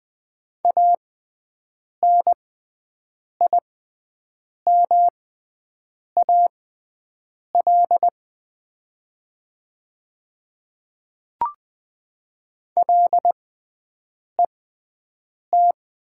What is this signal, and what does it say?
Morse code, heard as a keyed tone, about 20 wpm, spaced as for 7 wpm, 700 Hz. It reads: ANIMAL LET
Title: 6x Character Spacing - Top 300 Words - 20wpm